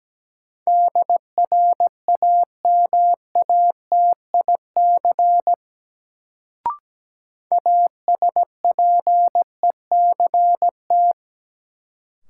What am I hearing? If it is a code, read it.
Morse code, heard as a keyed tone, about 17 wpm, 700 Hz. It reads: DRAMATIC ASPECT
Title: Sentences from Top 1300 Words - New Word Drill Alt - 17wpm